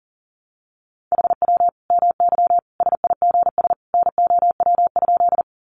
Morse code, heard as a keyed tone, about 40 wpm, 700 Hz. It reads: HW MY SIGS NOW?
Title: Common QSO Phrases 40wpm